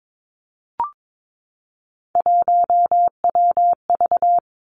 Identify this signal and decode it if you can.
Morse code, heard as a keyed tone, about 22 wpm, 700 Hz. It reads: 1WV
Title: Three Challenging Letter-Number Combinations 22wpm